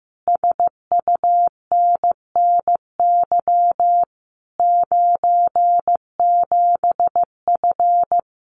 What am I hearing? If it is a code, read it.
Morse code, heard as a keyed tone, about 15 wpm, 700 Hz. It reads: SUNNY 97F
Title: Common QSO Phrases Encore 15wpm